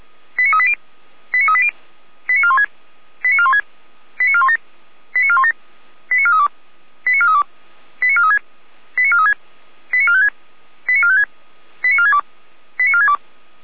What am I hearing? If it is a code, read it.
Five-tone selective call (ZVEI), heard as quick runs of five tones, tones of about 70 ms. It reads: 89380 89380 89416 89416 89417 89417 89432 89432 89436 89436 89456 89456 89462 89462
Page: SDR/ZVEI